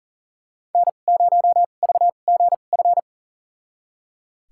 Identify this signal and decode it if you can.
Morse code, heard as a keyed tone, about 40 wpm, spaced as for 29 wpm, 700 Hz. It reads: N0VGF
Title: USA Call Signs - 2x Character Spacing - 40wpm